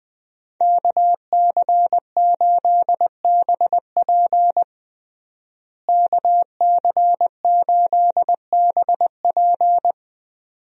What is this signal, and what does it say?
Morse code, heard as a keyed tone, about 20 wpm, 700 Hz. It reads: KC8BP KC8BP
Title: USA Call Signs - 5 Characters - Repeated 2x - 20wpm